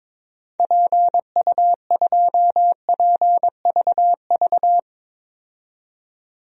Morse code, heard as a keyed tone, about 22 wpm, 700 Hz. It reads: PU2PVV
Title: DXCC-100 Call Signs - 6 Characters - 22wpm